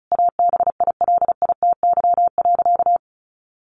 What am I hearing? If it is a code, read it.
Morse code, heard as a keyed tone, about 34 wpm, 700 Hz. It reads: ABILITY.
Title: Sentences from Top 1200 Words 35wpm